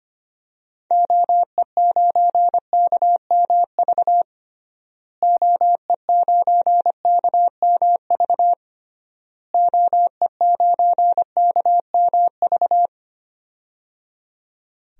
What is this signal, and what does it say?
Morse code, heard as a keyed tone, about 25 wpm, 700 Hz. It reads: OE9KMV OE9KMV OE9KMV
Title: DXCC-100 Call Signs - 6 Characters - Repeated 3x - 25wpm